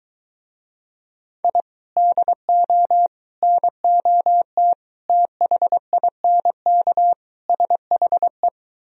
Morse code, heard as a keyed tone, about 23 wpm, 700 Hz. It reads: I DO NOT THINK SHE